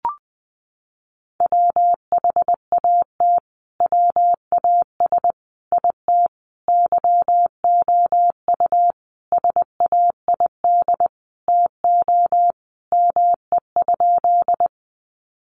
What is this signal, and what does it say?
Morse code, heard as a keyed tone, about 20 wpm, 700 Hz. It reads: WHAT WAS IT YOU SAID TO ME?
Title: Sentences from Top 200 Words 20wpm